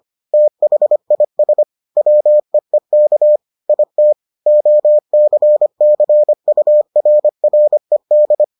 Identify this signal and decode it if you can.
Morse code, heard as a keyed tone, about 25 wpm, 600 Hz. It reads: THIS WEEK IT OCCURRED